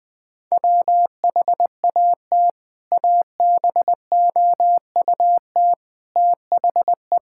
Morse code, heard as a keyed tone, about 20 wpm, 700 Hz. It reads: WHAT ABOUT THE
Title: Sentences from Top 1600 Words 20wpm